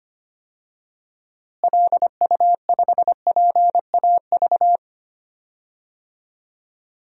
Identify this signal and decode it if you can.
Morse code, heard as a keyed tone, about 25 wpm, 700 Hz. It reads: LU5PAV
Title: DXCC-100 Call Signs - 6 Characters - Repeated 2x - 25wpm